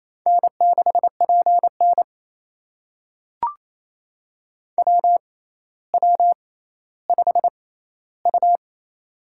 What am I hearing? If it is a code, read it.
Morse code, heard as a keyed tone, about 28 wpm, spaced as for 14 wpm, 700 Hz. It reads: N6PN WW5U
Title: USA Call Signs - 4 Characters - 6x Character Spacing - 28wpm